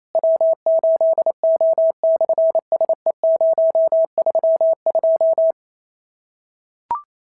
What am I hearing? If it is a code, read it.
Morse code, heard as a keyed tone, about 27 wpm, 650 Hz. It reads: W8O/SE032
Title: SOTA QSO Practice 28wpm